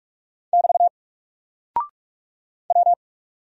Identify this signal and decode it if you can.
Morse code, heard as a keyed tone, about 44 wpm, 700 Hz. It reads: <BT> W
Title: Single Character (Letter:Number:Punctuation:Prosign) Alt 45wpm